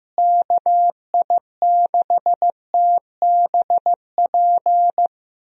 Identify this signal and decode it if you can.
Morse code, heard as a keyed tone, about 15 wpm, 700 Hz. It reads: KI6TBP
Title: USA Call Signs - 6 Characters - Repeated 2x - 15wpm